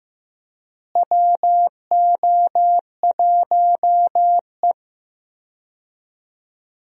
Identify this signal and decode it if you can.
Morse code, heard as a keyed tone, about 15 wpm, 700 Hz. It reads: WO1E
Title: USA Call Signs - 4 Characters - Repeated 2x - 15wpm